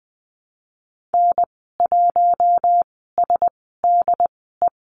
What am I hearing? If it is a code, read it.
Morse code, heard as a keyed tone, about 20 wpm, spaced as for 15 wpm, 700 Hz. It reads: N1SDE
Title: 076 Lesson 13g - Letter L - Call Signs 20wpm